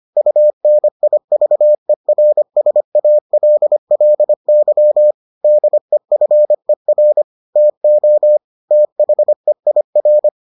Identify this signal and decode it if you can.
Morse code, heard as a keyed tone, about 25 wpm, 600 Hz. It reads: UNIVERSALLY DEFER TO THEIR